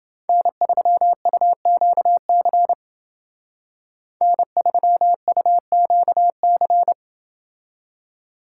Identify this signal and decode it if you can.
Morse code, heard as a keyed tone, about 29 wpm, 700 Hz. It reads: N3UQC N3UQC